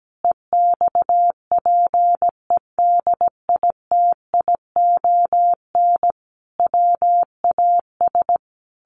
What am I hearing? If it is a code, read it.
Morse code, heard as a keyed tone, about 17 wpm, 700 Hz. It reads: EXPEDITION WAS
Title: A Sound Of Thunder 17wpm